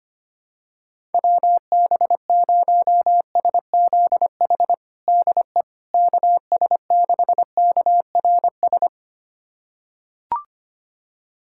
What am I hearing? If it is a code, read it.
Morse code, heard as a keyed tone, about 25 wpm, 700 Hz. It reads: WB0SZH DE KS6KRS
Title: USA Callsign DE Callsign 25wpm